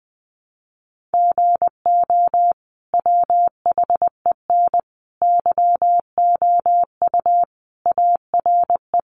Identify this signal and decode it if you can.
Morse code, heard as a keyed tone, about 19 wpm, 700 Hz. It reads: GO WHEN YOU ARE
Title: Sentences from Top 500 Words 20wpm